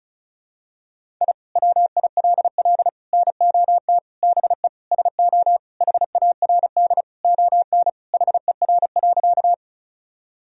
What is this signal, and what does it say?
Morse code, heard as a keyed tone, about 34 wpm, 700 Hz. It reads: I WILL NOT BE SO HARD ON HER.
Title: Sentences from Top 1000 Words - Review Part 1 - 35wpm